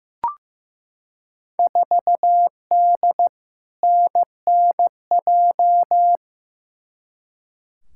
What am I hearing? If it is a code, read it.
Morse code, heard as a keyed tone, about 15 wpm, 700 Hz. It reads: 4D NNJ
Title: Field Day Exchanges Encore - 15wpm